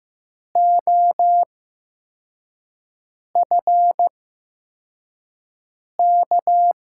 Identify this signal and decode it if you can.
Morse code, heard as a keyed tone, about 15 wpm, spaced as for 4 wpm, 700 Hz. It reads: OFK
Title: USA Call Signs - 5 Characters - 8x Character Spacing - 15wpm